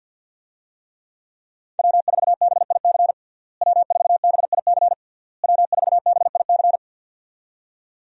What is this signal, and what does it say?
Morse code, heard as a keyed tone, about 52 wpm, 700 Hz. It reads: W4BIC W4BIC W4BIC
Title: USA Call Signs - 5 Characters - Repeated 3x - 50wpm